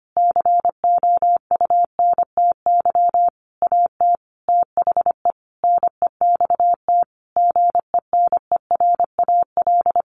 Morse code, heard as a keyed tone, about 25 wpm, 700 Hz. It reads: COUNTY AT THE NEXT GENERAL
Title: Sentences from Top 1500 Words 25wpm